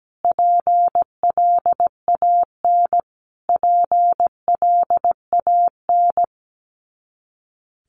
Morse code, heard as a keyed tone, about 17 wpm, 700 Hz. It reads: PLAN PLAN
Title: Top 500 Words Repeated 17wpm